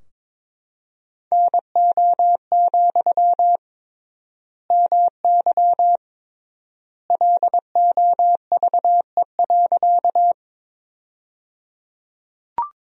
Morse code, heard as a keyed tone, about 22 wpm, 700 Hz. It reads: NO, MY LOVE.
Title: 3x Word-Spacing - Sentences from Top 400 Words - 22wpm